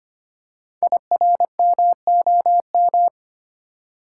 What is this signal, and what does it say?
Morse code, heard as a keyed tone, about 25 wpm, 700 Hz. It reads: IRMOM
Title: Code Groups - 5 Letters - 25wpm